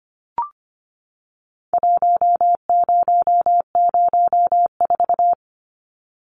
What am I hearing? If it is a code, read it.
Morse code, heard as a keyed tone, about 25 wpm, 700 Hz. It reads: 1004
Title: Numbers - Contest Style - 4-Digit 25wpm